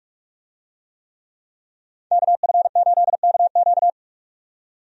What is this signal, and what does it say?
Morse code, heard as a keyed tone, about 46 wpm, 700 Hz. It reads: KF8KQ
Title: USA Call Signs - 5 Characters - 45wpm